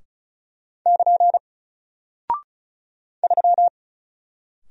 Morse code, heard as a keyed tone, about 36 wpm, 700 Hz. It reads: <KN> 3
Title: Single Character (Letter:Number:Punctuation:Prosign) Alt 35wpm